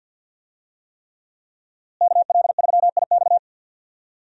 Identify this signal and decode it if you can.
Morse code, heard as a keyed tone, about 52 wpm, 700 Hz. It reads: KL3IX
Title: USA Call Signs - 5 Characters - Repeated 3x - 50wpm